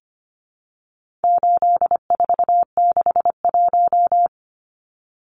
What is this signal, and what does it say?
Morse code, heard as a keyed tone, about 25 wpm, 700 Hz. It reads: 8461
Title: Numbers - 4-Digit 25wpm